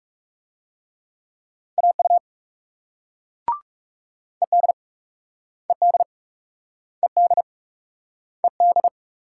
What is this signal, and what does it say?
Morse code, heard as a keyed tone, about 41 wpm, 700 Hz. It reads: AU ED ED ED ED